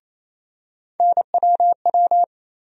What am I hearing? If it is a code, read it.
Morse code, heard as a keyed tone, about 27 wpm, 700 Hz. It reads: NWW
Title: USA Call Sign Postfixes - 3 Letters 28wpm